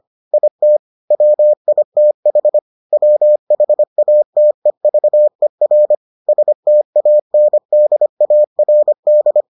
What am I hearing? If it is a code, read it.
Morse code, heard as a keyed tone, about 25 wpm, 600 Hz. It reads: IT WITH WHATEVER STANDARD